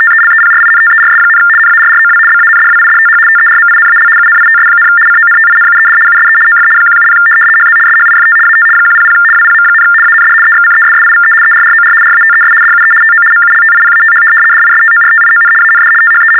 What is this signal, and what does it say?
RTTY, about 75 baud. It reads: TELEPRINTERS, NARROW BANDWIDTH TERMINAL UNITS, CONTINUED USE OF 170 HZ SHIFT AT 100 WPM, AND EXCESSIVE ERROR RATES DUE TO MULTIPATH DISTORTION AND THE
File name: teletype-RTTY-75